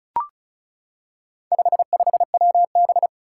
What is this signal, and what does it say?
Morse code, heard as a keyed tone, about 36 wpm, 700 Hz. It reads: 55WB
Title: Four Challenging Letter-Number Combinations 35wpm